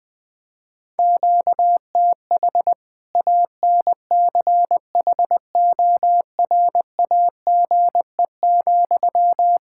Morse code, heard as a keyed tone, about 20 wpm, 700 Hz. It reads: QTH ANCHORAGE,
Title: Practice QSO - Exchange 1 - Beginning - 20wpm